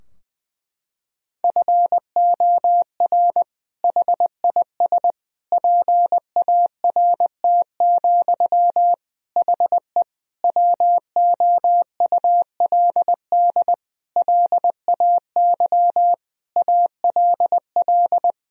Morse code, heard as a keyed tone, about 20 wpm, 700 Hz. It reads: FOR HIS PART, HE WOULD LAY ALL
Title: Sentences from Top 1000 Words - Review Part 2 - 20wpm